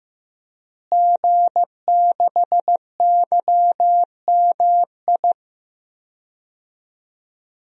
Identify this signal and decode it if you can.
Morse code, heard as a keyed tone, about 15 wpm, 700 Hz. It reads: G6YMI
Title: DXCC-100 Call Signs - 5 Characters - Repeated 2x - 15wpm